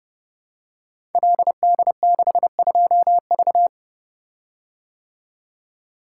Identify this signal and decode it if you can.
Morse code, heard as a keyed tone, about 30 wpm, 700 Hz. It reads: LD62V
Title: Code Groups - 5 Letter:Numbers - 30wpm